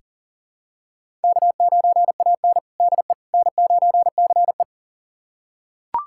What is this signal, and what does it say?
Morse code, heard as a keyed tone, about 40 wpm, 700 Hz. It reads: K9AN DE N9CE
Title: USA Callsign DE Callsign - 4 Characters - 40wpm